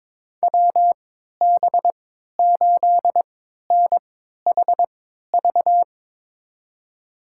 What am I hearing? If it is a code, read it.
Morse code, heard as a keyed tone, about 23 wpm, spaced as for 13 wpm, 700 Hz. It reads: WB8NHV
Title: USA Call Signs - 3x Character Spacing - 22wpm